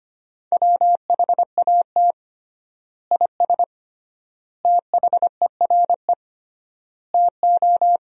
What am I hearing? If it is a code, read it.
Morse code, heard as a keyed tone, about 25 wpm, 700 Hz. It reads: WHAT IS THERE TO
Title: 3x Word-Spacing - Sentences from Top 100 Words - 25wpm